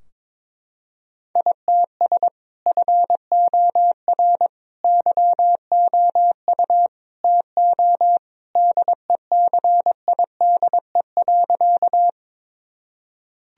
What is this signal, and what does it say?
Morse code, heard as a keyed tone, about 22 wpm, 700 Hz. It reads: ITS FOR YOU TO DECIDE.
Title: Sentences from Top 1000 Words - Review Part 3 - 22wpm